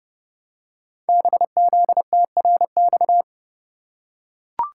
Morse code, heard as a keyed tone, about 30 wpm, 700 Hz. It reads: BZTRX
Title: Code Groups - 5 Letter:Numbers - 30wpm